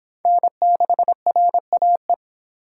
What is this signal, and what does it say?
Morse code, heard as a keyed tone, about 28 wpm, 700 Hz. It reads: N6RAE